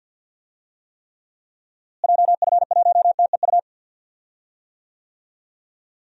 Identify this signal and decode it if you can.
Morse code, heard as a keyed tone, about 49 wpm, 700 Hz. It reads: JF1TEU
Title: DXCC-100 Call Signs - 6 Characters - Repeated 2x - 50wpm